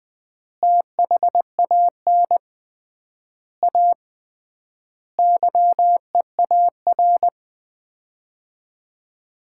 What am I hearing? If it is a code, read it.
Morse code, heard as a keyed tone, about 21 wpm, 700 Hz. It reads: THAN A YEAR
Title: 3x Word-Spacing - Sets of 5 Words - 20wpm